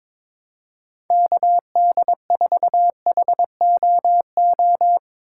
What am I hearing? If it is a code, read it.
Morse code, heard as a keyed tone, about 22 wpm, 700 Hz. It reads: KD4HOO
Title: USA Call Signs - 6 Characters - 3x Character Spacing - 22wpm